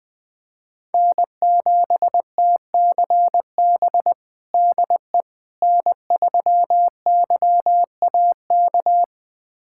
Morse code, heard as a keyed tone, about 21 wpm, 700 Hz. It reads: N7TCB DE N3YAK
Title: USA Callsign DE Callsign - 5 Characters - 20wpm